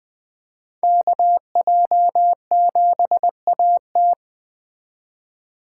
Morse code, heard as a keyed tone, about 20 wpm, 700 Hz. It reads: KJ7AT
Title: Speed Racing - USA 2x2 Call Signs 17wpm